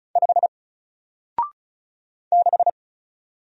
Morse code, heard as a keyed tone, about 37 wpm, 700 Hz. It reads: 5 6
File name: Single-Letter-Number-35wpm